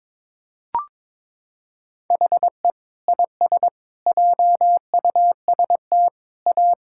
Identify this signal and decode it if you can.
Morse code, heard as a keyed tone, about 22 wpm, 700 Hz. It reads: HE IS JUST A
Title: Sentences from Top 1000 Words - Review Part 6 - 22wpm